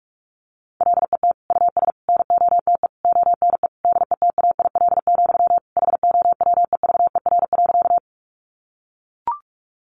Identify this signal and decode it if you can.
Morse code, heard as a keyed tone, about 44 wpm, 700 Hz. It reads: LET US NOTE ONE DETAIL, HOWEVER.
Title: Sentences from Top 1500 Words 45wpm